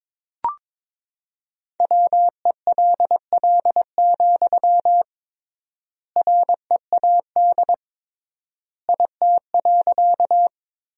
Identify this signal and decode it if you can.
Morse code, heard as a keyed tone, about 22 wpm, 700 Hz. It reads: WELL, READ IT.
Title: 3x Word-Spacing - Sentences from Top 200 Words - 22wpm